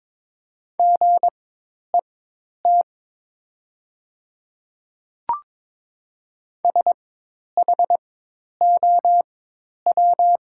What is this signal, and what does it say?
Morse code, heard as a keyed tone, about 22 wpm, spaced as for 10 wpm, 700 Hz. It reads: GET SHOW
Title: 4x Character Spacing - Top 200 Words - 22wpm